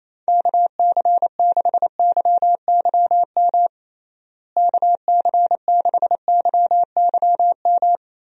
Morse code, heard as a keyed tone, about 28 wpm, 700 Hz. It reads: KC6YYM KC6YYM
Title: USA Call Signs - 6 Characters - Repeated 2x - 28wpm